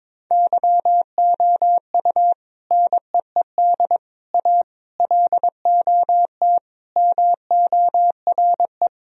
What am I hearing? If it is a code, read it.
Morse code, heard as a keyed tone, about 22 wpm, 700 Hz. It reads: YOU NEED A LOT MORE
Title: Sentences from Top 1000 Words - Review Part 3 - 22wpm